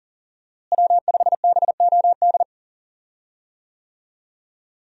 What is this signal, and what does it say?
Morse code, heard as a keyed tone, about 40 wpm, 700 Hz. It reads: W5BOD
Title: USA Call Signs - 5 Characters - Repeated 3x - 40wpm